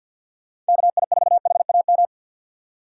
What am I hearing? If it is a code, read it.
Morse code, heard as a keyed tone, about 46 wpm, 700 Hz. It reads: KI4HAM
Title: USA Call Signs - 6 Characters - 4x Character Spacing - 50wpm